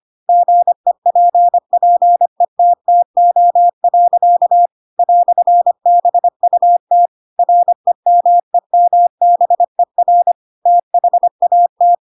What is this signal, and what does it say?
Morse code, heard as a keyed tone, about 25 wpm, 700 Hz. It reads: GEPPETTO. "BUT REMEMBER THAT